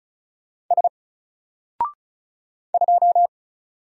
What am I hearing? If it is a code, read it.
Morse code, heard as a keyed tone, about 34 wpm, 700 Hz. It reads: S 2
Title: Single Letter-Number 35wpm